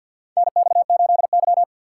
Morse code, heard as a keyed tone, about 53 wpm, 700 Hz. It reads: NX8Y
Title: USA Call Signs - 6x Character Spacing - 50wpm